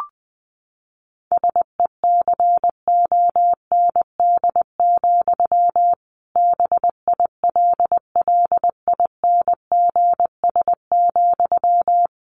Morse code, heard as a keyed tone, about 20 wpm, 700 Hz. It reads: SECOND, BILLINGS,
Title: A Sound Of Thunder 20wpm